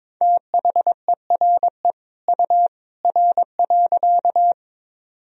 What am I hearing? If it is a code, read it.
Morse code, heard as a keyed tone, about 22 wpm, 700 Hz. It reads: THERE U R.
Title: Sentences from Top 100 Words and Abbreviations 22wpm